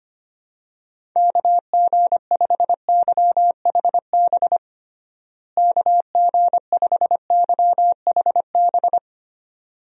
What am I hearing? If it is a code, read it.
Morse code, heard as a keyed tone, about 25 wpm, 700 Hz. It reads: KG5YHB KG5YHB
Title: USA Call Signs - 6 Characters - Repeated 2x - 25wpm